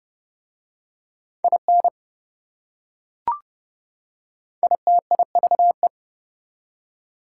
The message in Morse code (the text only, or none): IN ITIVE